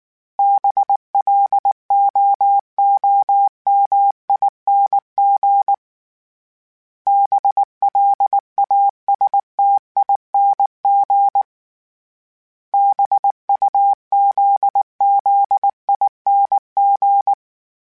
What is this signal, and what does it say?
Morse code, heard as a keyed tone, about 19 wpm, 800 Hz. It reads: BLOOMING BLASTING BUZZING